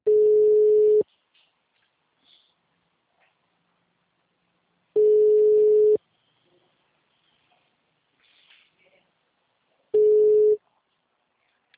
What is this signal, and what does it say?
Ringback tone (Europe, 425 Hz).
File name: out- -2002- - - .11908